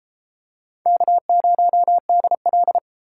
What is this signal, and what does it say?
Morse code, heard as a keyed tone, about 33 wpm, 700 Hz. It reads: K0DL